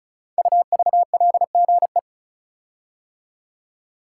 Morse code, heard as a keyed tone, about 34 wpm, 700 Hz. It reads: UVLGE
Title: Code Groups - 5 Letter:Numbers - 35wpm